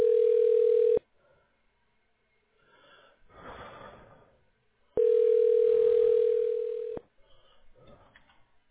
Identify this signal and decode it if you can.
Ringback tone (North America), 440 + 480 Hz.